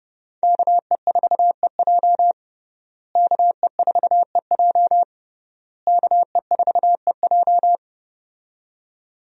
Morse code, heard as a keyed tone, about 32 wpm, 700 Hz. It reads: KE4EJ KE4EJ KE4EJ
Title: USA Call Signs - 5 Characters - Repeated 3x - 30wpm